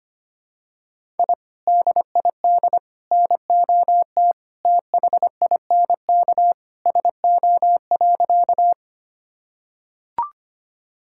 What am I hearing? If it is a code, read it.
Morse code, heard as a keyed tone, about 25 wpm, 700 Hz. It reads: I DID NOT THINK SO.
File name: practice / 25wpm / Sentences with 2-Word Phrases - 25wpm